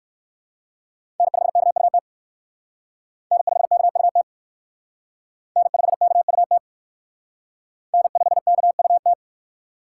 Morse code, heard as a keyed tone, about 55 wpm, 700 Hz. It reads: N5KUT N5KUT N5KUT N5KUT
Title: Speed Racing - USA 1x3 Call Signs 45wpm